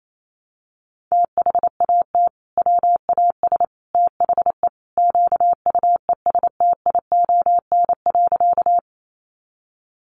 Morse code, heard as a keyed tone, about 28 wpm, 700 Hz. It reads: THAT WAS THE QUESTION.
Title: Sentences from Top 400 Words 28wpm